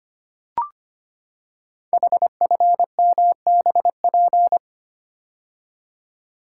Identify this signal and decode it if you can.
Morse code, heard as a keyed tone, about 25 wpm, 700 Hz. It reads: HFMBP
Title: Code Groups - 5 Letter:Numbers - 25wpm